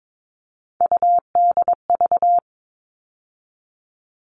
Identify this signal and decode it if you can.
Morse code, heard as a keyed tone, about 22 wpm, 700 Hz. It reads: UDV
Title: All Cut Numbers - 3-Digit 22wpm